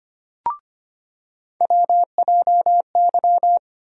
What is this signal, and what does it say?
Morse code, heard as a keyed tone, about 25 wpm, 700 Hz. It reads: WJY
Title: USA Call Sign Postfixes - 3 Letters 25wpm